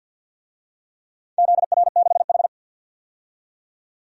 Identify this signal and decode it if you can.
Morse code, heard as a keyed tone, about 47 wpm, 700 Hz. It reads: ZR6H
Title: DXCC-100 Call Signs - 4 Characters - 50wpm